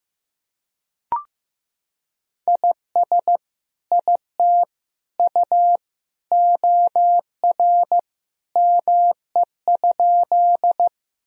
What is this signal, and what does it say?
Morse code, heard as a keyed tone, about 15 wpm, 700 Hz. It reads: IS IT U OR ME?
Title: Sentences from Top 200 Words and Abbreviations 15wpm